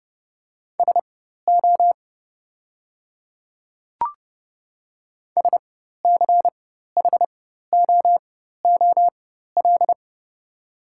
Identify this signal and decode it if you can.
Morse code, heard as a keyed tone, about 30 wpm, spaced as for 14 wpm, 690 Hz. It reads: SO SCHOOL
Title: 4x Character Spacing - Top 500 Words - 30wpm